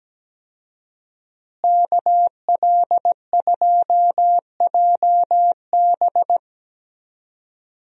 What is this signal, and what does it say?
Morse code, heard as a keyed tone, about 17 wpm, 700 Hz. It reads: KL2JB
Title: USA Call Signs - 5 Characters - Repeated 3x - 17wpm